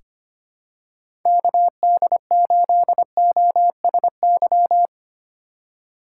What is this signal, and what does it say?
Morse code, heard as a keyed tone, about 25 wpm, 700 Hz. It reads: KD8OSY
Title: USA Call Signs - 6 Characters - 3x Character Spacing - 25wpm